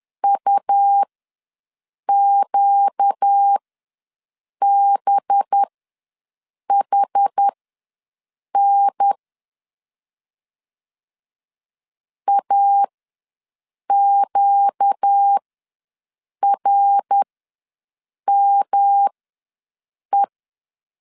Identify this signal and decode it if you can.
Morse code, heard as a keyed tone, about 10 wpm, spaced as for 6 wpm, 800 Hz. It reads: UQBHN AQRME
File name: LEZIONI / 5 LBVC / 5 LB ws 10wpm 3616